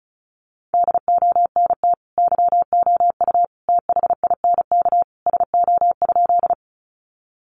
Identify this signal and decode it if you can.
Morse code, heard as a keyed tone, about 34 wpm, 700 Hz. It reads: DONT YOU THINK SO?
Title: Sentences from Top 300 Words 35wpm